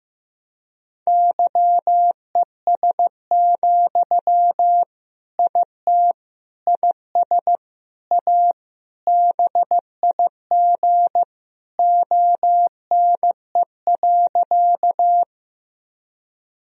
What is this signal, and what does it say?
Morse code, heard as a keyed tone, about 15 wpm, 700 Hz. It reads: YES, IT IS A BIG ONE.
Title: Sentences from Top 1000 Words - Review Part 1 - 15wpm